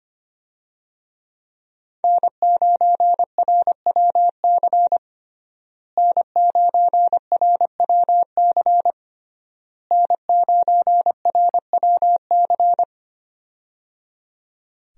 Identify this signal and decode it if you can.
Morse code, heard as a keyed tone, about 25 wpm, 700 Hz. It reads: N9RWC N9RWC N9RWC
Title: USA Call Signs - 5 Characters - Repeated 3x - 25wpm